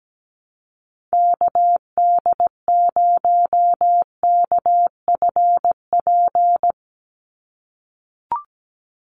Morse code, heard as a keyed tone, about 17 wpm, 700 Hz. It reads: KD0KFP
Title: USA Call Signs 17wpm